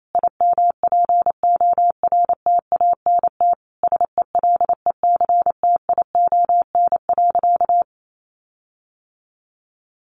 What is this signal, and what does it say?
Morse code, heard as a keyed tone, about 28 wpm, 700 Hz. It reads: IMPORTANT SELECTION.
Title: Sentences from Top 1400 Words 28wpm